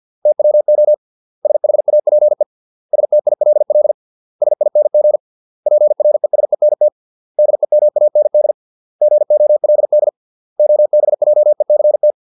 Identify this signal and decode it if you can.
Morse code, heard as a keyed tone, about 50 wpm, 600 Hz. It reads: TWO SHAPE STILL SING PRESENT DEMAND GOLD OBJECT